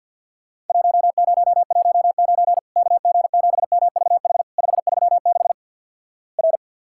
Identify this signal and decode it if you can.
Morse code, heard as a keyed tone, about 50 wpm, 700 Hz. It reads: 1019 KG7MVH 536 R
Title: Worked All Europe DX Contest - QTC Exchange with Repeats - 50wpm